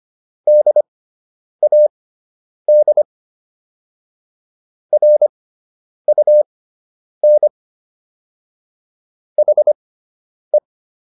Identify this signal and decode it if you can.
Morse code, heard as a keyed tone, about 25 wpm, spaced as for 9 wpm, 600 Hz. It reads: DAD RUN HE